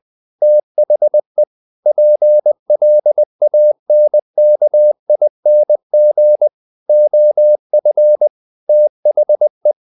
Morse code, heard as a keyed tone, about 20 wpm, 600 Hz. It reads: THE PLANKING OF THE
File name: the old man and the sea part 2 20wpm